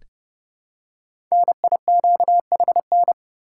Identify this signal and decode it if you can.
Morse code, heard as a keyed tone, about 30 wpm, 700 Hz. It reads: NIQHN
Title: Code Groups - 5 Letter:Numbers - 30wpm